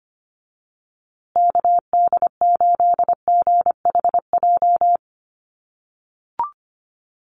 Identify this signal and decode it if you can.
Morse code, heard as a keyed tone, about 25 wpm, 700 Hz. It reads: KD8GHJ
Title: USA Call Signs Repeated 2x 25wpm